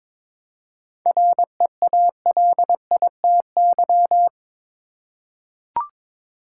Morse code, heard as a keyed tone, about 22 wpm, 700 Hz. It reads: REALITY
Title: Seven Letter Words - 22wpm